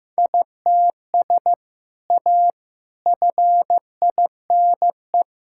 Morse code, heard as a keyed tone, about 15 wpm, 700 Hz. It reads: ITS A FINE
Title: Sentences from Top 1000 Words - Review Part 6 - 15wpm